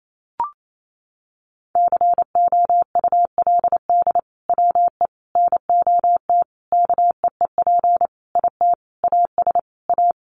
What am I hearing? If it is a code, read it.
Morse code, heard as a keyed tone, about 28 wpm, 700 Hz. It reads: COULD WE NOT KEEP IT AS A
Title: Sentences from Top 1200 Words 28wpm